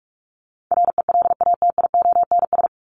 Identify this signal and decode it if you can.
Morse code, heard as a keyed tone, about 44 wpm, 700 Hz. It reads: RELATIONS
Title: ES:S Suffix - Top 100 Words with Base Words 45wpm